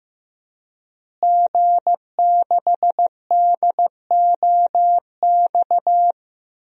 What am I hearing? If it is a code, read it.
Morse code, heard as a keyed tone, about 14 wpm, 700 Hz. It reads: G6DOX
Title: DXCC-100 Call Signs - 5 Characters - Repeated 3x - 15wpm